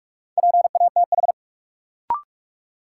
Morse code, heard as a keyed tone, about 46 wpm, 700 Hz. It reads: PATH
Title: Sentences from Top 1400 Words - New Word Drill - 45wpm